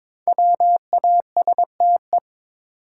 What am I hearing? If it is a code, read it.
Morse code, heard as a keyed tone, about 22 wpm, 690 Hz. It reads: WASTE
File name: 5 Letter Words - First 3 Letters Given - 22wpm